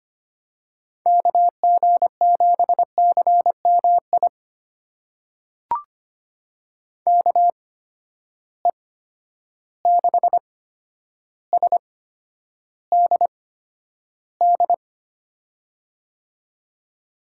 Morse code, heard as a keyed tone, about 25 wpm, spaced as for 11 wpm, 700 Hz. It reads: KG7CMI KE6SDD
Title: USA Call Signs - 6 Characters - 8x Character Spacing - 25wpm